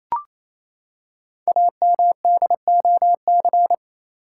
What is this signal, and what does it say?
Morse code, heard as a keyed tone, about 27 wpm, 700 Hz. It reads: AMDOC
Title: Code Groups - 5 Letters - 28wpm